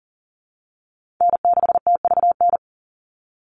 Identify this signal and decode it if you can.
Morse code, heard as a keyed tone, about 40 wpm, 700 Hz. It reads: N6TVN
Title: Call Signs Repeated 2x - 40wpm